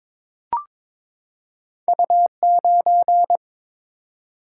Morse code, heard as a keyed tone, about 22 wpm, 690 Hz. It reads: U9
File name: practice / 22wpm / Two Letter-Number Combinations - 22wpm